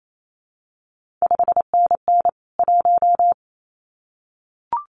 Morse code, heard as a keyed tone, about 28 wpm, 700 Hz. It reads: 5NN 1
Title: CQ WW DX Contest Exchange - 28wpm